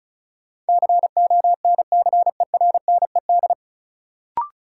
Morse code, heard as a keyed tone, about 36 wpm, 700 Hz. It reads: CONCERNED
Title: Sentences from Top 1000 Words - Review Part 6 - New Word Drill - 35wpm